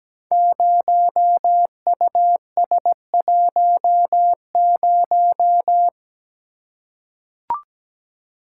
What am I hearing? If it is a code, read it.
Morse code, heard as a keyed tone, about 17 wpm, 700 Hz. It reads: 0US10
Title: Code Groups - 5 Letter:Numbers - 17wpm